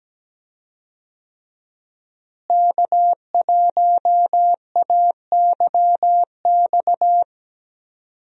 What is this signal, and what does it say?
Morse code, heard as a keyed tone, about 17 wpm, 700 Hz. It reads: K1AYX
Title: USA Call Signs - 5 Characters - Repeated 3x - 17wpm